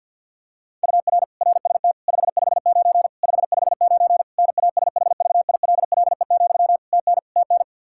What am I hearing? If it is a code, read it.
Morse code, heard as a keyed tone, about 51 wpm, 700 Hz. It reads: UR RST 559 559 NASHVILLE, TN TN
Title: Practice QSO - Exchange 1 - Beginning - 50wpm